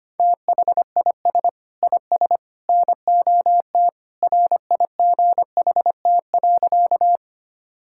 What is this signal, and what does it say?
Morse code, heard as a keyed tone, about 25 wpm, 700 Hz. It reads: THIS IS NOT RIGHT.
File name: practice / 25wpm / Sentences with 3-Word Phrases - 25wpm